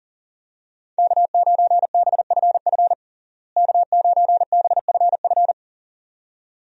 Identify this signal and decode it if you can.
Morse code, heard as a keyed tone, about 40 wpm, 700 Hz. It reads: K9BFF K9BFF
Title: USA Call Signs - 5 Characters - Repeated 2x - 40wpm